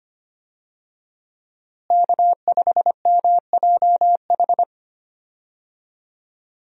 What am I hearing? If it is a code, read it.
Morse code, heard as a keyed tone, about 25 wpm, 700 Hz. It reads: K5MJH
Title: USA Call Signs - 5 Characters - Repeated 2x - 25wpm